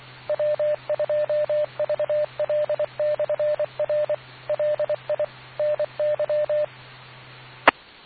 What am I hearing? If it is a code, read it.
Morse code, heard as a keyed tone, about 24 wpm, 600 Hz. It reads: W2VL/R LI NY E